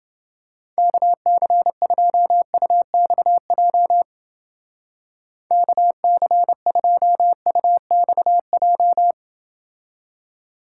Morse code, heard as a keyed tone, about 29 wpm, 700 Hz. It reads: KC2UXJ KC2UXJ